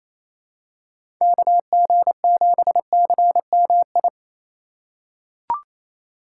Morse code, heard as a keyed tone, about 28 wpm, 700 Hz. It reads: KG7CMI
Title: USA Call Signs - 4x Character Spacing - 28wpm